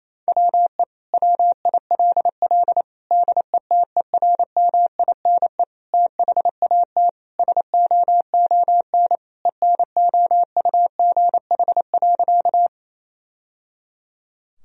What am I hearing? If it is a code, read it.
Morse code, heard as a keyed tone, about 28 wpm, 700 Hz. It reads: WE WILL DETERMINE THAT SOON ENOUGH.
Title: Sentences from Top 1000 Words - Review Part 4 - 28wpm